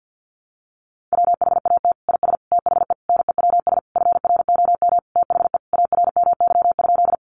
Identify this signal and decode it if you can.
Morse code, heard as a keyed tone, about 50 wpm, 700 Hz. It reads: WHAT IS THE NEWS FROM THE ARMY?
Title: Sentences from Top 800 Words 50wpm